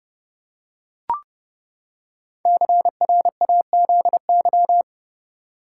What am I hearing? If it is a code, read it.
Morse code, heard as a keyed tone, about 29 wpm, 700 Hz. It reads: CRAZY
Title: Five Letter Words - 30wpm